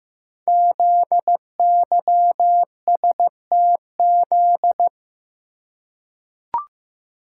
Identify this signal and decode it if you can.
Morse code, heard as a keyed tone, about 15 wpm, 700 Hz. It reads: ZYSTZ
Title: Code Groups - 5 Letters - 15wpm